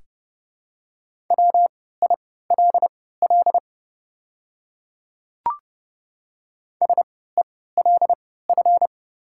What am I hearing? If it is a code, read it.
Morse code, heard as a keyed tone, about 31 wpm, spaced as for 17 wpm, 700 Hz. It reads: WILL SELF
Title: 3x Character Spacing - Top 200 Words - 30wpm